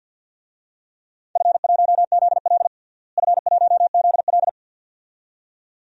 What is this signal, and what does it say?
Morse code, heard as a keyed tone, about 53 wpm, 700 Hz. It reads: F1ZL F1ZL
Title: DXCC-100 Call Signs - 4 Characters - Repeated 2x - 50wpm